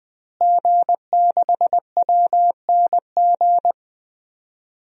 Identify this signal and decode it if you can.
Morse code, heard as a keyed tone, about 21 wpm, 700 Hz. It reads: G6WNG
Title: DXCC-100 Call Signs - 5 Characters - Repeated 3x - 20wpm